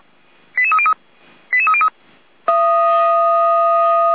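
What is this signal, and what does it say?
Five-tone selective call (ZVEI), heard as quick runs of five tones, tones of about 70 ms. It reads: 88303 88303
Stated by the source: ZVEI five-tone call received by SDR, from the audio.f85360.feuerwehr-orlamuende.org fire brigade (Feuerwehr Orlamünde) server.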